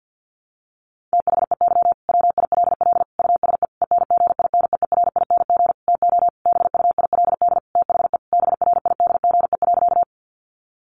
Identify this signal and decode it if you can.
Morse code, heard as a keyed tone, about 50 wpm, 700 Hz. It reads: THEY WILL USE ENGINEERING TO BUILD THE BRIDGE.
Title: Sentences from Top 1500 Words 50wpm